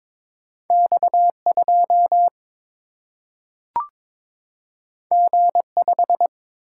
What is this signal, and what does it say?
Morse code, heard as a keyed tone, about 22 wpm, 700 Hz. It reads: X2 G5
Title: Single-Letter Single-Number 22wpm